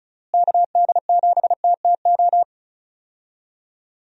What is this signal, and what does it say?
Morse code, heard as a keyed tone, about 36 wpm, 700 Hz. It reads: KD7TTO
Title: USA Call Signs - 6 Characters - Repeated 3x - 35wpm